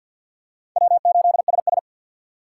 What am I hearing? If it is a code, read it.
Morse code, heard as a keyed tone, about 50 wpm, 700 Hz. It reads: W8SS
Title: USA Call Signs - 3x Character Spacing - 50wpm